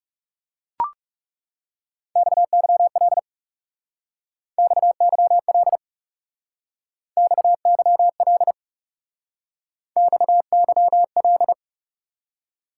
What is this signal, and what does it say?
Morse code, heard as a keyed tone, about 38 wpm, 700 Hz. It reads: XYL XYL XYL XYL